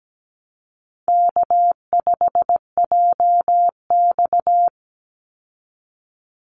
Morse code, heard as a keyed tone, about 17 wpm, 700 Hz. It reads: K5JX
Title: Easy Call Signs Repeated 2x - 17wpm